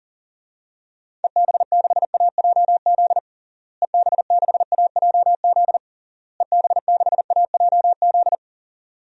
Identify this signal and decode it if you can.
Morse code, heard as a keyed tone, about 38 wpm, 700 Hz. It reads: EB6AJZ EB6AJZ EB6AJZ
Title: DXCC-100 Call Signs - 6 Characters - Repeated 3x - 40wpm